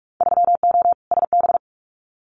Morse code, heard as a keyed tone, about 46 wpm, 700 Hz. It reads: 3O SB
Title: Winter Field Day Exchanges Encore 45wpm